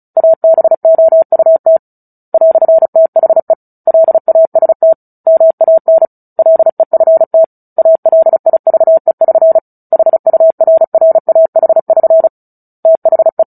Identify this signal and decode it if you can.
Morse code, heard as a keyed tone, about 34 wpm, 650 Hz. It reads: ABOUT "THE LAST MAN LEFT ALIVE<SN> HURRAH<SN> THE
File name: Book Two / 09-Chapter 9 - 35 WPM